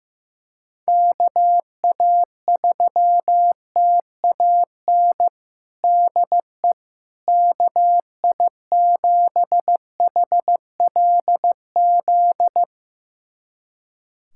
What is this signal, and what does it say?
Morse code, heard as a keyed tone, about 15 wpm, 700 Hz. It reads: KA3TAN DE KI7HLZ
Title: USA Callsign DE Callsign - 6 Characters - 15wpm